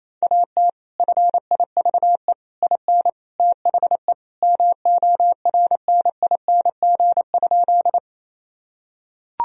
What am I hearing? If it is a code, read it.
Morse code, heard as a keyed tone, about 28 wpm, 700 Hz. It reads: AT FIVE IN THE MORNING?
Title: Sentences from Top 1000 Words - Review Part 2 - 28wpm